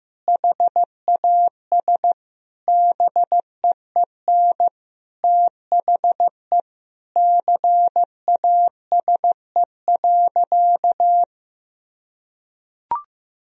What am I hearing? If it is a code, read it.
Morse code, heard as a keyed tone, about 15 wpm, 700 Hz. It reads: HAS BEEN THE CASE.
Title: Sentences from Top 1000 Words - Review Part 3 - 15wpm